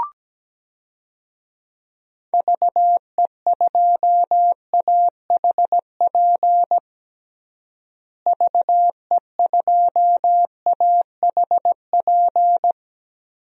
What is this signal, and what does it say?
Morse code, heard as a keyed tone, about 17 wpm, 700 Hz. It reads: VE2AHP VE2AHP
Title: DXCC-100 Call Signs - 6 Characters - Repeated 2x - 17wpm